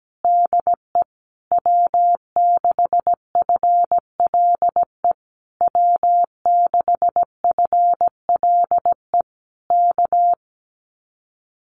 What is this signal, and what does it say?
Morse code, heard as a keyed tone, about 17 wpm, 700 Hz. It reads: DE W6FLE W6FLE K
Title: Common QSO Phrases Encore 17wpm